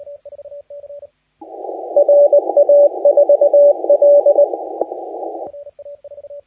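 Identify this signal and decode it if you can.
Morse code, heard as a keyed tone, about 19 wpm, 560 Hz, with radio noise. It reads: RA4L
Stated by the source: Morse contest audio recorded by LA4C during the SAC CW 2018 contest.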